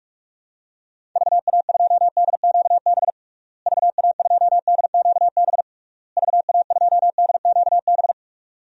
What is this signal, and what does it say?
Morse code, heard as a keyed tone, about 46 wpm, 700 Hz. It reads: VA2DQB VA2DQB VA2DQB
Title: DXCC-100 Call Signs - 6 Characters - Repeated 3x - 45wpm